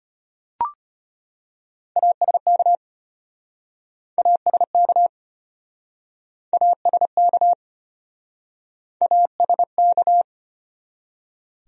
Morse code, heard as a keyed tone, about 34 wpm, 700 Hz. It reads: ASK ASK ASK ASK